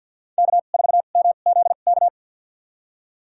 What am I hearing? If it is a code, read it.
Morse code, heard as a keyed tone, about 51 wpm, spaced as for 36 wpm, 700 Hz. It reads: K4MZK
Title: USA Call Signs - 5 Characters - 2x Character Spacing - 50wpm